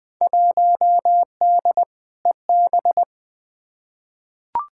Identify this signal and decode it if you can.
Morse code, heard as a keyed tone, about 20 wpm, 700 Hz. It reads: 1D EB
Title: Field Day Exchanges - 20wpm